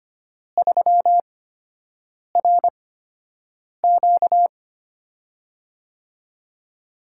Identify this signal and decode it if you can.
Morse code, heard as a keyed tone, about 25 wpm, spaced as for 7 wpm, 700 Hz. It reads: 3RQ
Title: USA Call Signs - 5 Characters - 8x Character Spacing - 25wpm